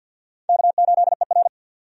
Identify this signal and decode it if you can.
Morse code, heard as a keyed tone, about 49 wpm, 700 Hz. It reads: K8ER
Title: USA Call Signs - 4 Characters - 50wpm